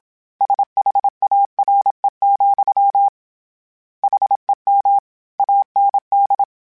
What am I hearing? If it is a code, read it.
Morse code, heard as a keyed tone, about 27 wpm, 800 Hz. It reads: SHARE, HEM AND
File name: Misc / Misc-mp3s / Word Pairs @25